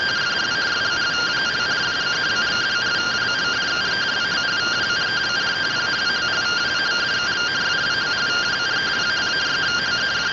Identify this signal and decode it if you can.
Two-tone teleprinter signal, about 170 Hz shift.
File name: findings / AXIAL fr45 3mm.gen-45 rtty rx